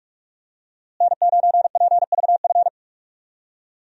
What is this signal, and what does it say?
Morse code, heard as a keyed tone, about 44 wpm, 700 Hz. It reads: N9PVF
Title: USA Call Signs - 5 Characters - Repeated 2x - 45wpm